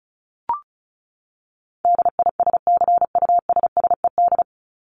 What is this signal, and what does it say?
Morse code, heard as a keyed tone, about 34 wpm, 700 Hz. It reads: DISCUSSED
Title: ED Suffix - Top 100 Words with Base Words 35wpm